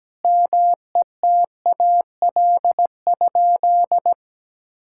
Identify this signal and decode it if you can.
Morse code, heard as a keyed tone, about 17 wpm, 700 Hz. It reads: METAL?
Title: Sentences from Top 1600 Words 17wpm